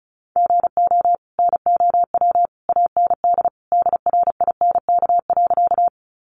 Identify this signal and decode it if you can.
Morse code, heard as a keyed tone, about 34 wpm, 700 Hz. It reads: GO NOW AND DRINK.
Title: Sentences from Top 1500 Words 35wpm